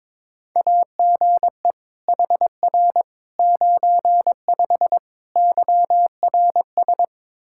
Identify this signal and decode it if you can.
Morse code, heard as a keyed tone, about 22 wpm, 700 Hz. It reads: AGE HR 95 YRS
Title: Practice QSO - Exchange 3 - More Info - 22wpm